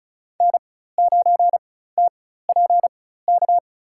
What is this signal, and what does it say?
Morse code, heard as a keyed tone, about 36 wpm, spaced as for 16 wpm, 700 Hz. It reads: N9TPK
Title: USA Call Signs - 5 Characters - 4x Character Spacing - 35wpm